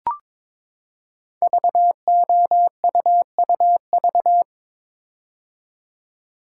Morse code, heard as a keyed tone, about 22 wpm, 700 Hz. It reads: VOUUV
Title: Code Groups - 5 Letters - 22wpm